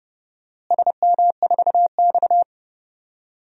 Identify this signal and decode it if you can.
Morse code, heard as a keyed tone, about 30 wpm, 700 Hz. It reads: SM4X
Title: DXCC-100 Call Signs - 4 Characters - Repeated 3x - 30wpm